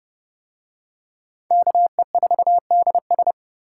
Morse code, heard as a keyed tone, about 31 wpm, 700 Hz. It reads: KE4DS